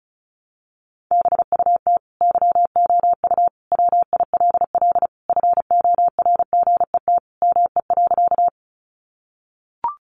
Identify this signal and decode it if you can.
Morse code, heard as a keyed tone, about 34 wpm, 700 Hz. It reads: BUT YOU WILL FORGET ME.
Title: Sentences from Top 1200 Words 35wpm